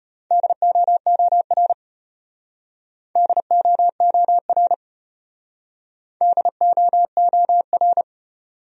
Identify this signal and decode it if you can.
Morse code, heard as a keyed tone, about 33 wpm, 700 Hz. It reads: DOOR DOOR DOOR